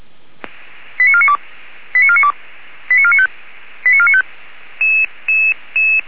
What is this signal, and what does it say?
Five-tone selective call (ZVEI), heard as quick runs of five tones, tones of about 70 ms. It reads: 89492 89492 89495 89495
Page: /464/SDR/ZVEI/